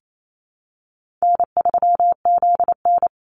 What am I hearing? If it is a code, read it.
Morse code, heard as a keyed tone, about 28 wpm, 700 Hz. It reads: N3ZN
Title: Call Signs - 28wpm